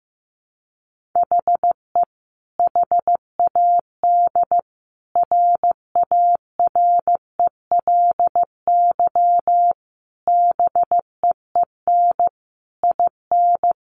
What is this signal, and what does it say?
Morse code, heard as a keyed tone, about 14 wpm, 700 Hz. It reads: HE HAD RARELY BEEN IN
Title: Sentences from Top 1400 Words 15wpm